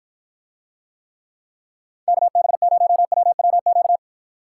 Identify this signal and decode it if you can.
Morse code, heard as a keyed tone, about 53 wpm, 700 Hz. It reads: KB0WWQ